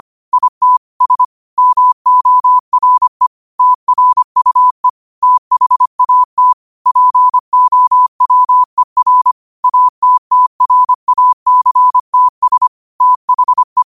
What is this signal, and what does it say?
Morse code, heard as a keyed tone, about 25 wpm, 1000 Hz. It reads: ITS MORE TRUE THAT POWER ATTRACTS THE
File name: 25wpm